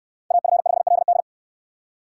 Morse code, heard as a keyed tone, about 68 wpm, 700 Hz. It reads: AF5LD